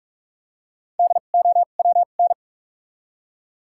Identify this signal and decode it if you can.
Morse code, heard as a keyed tone, about 46 wpm, spaced as for 33 wpm, 700 Hz. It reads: DOWN
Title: 2x Character Spacing - Top 300 Words - 45wpm